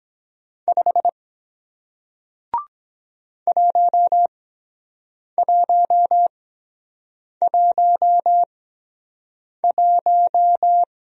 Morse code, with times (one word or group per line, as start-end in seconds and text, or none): 0.7-1.1 5
3.5-10.8 1111